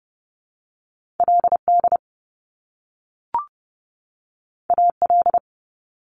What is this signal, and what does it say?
Morse code, heard as a keyed tone, about 30 wpm, 700 Hz. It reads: LD AL
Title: Top Two Letter Combinations in Words 30wpm